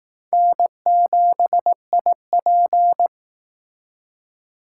Morse code, heard as a keyed tone, about 19 wpm, 700 Hz. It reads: N7IP
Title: Speed Racing - USA 1x2 Call Signs 15wpm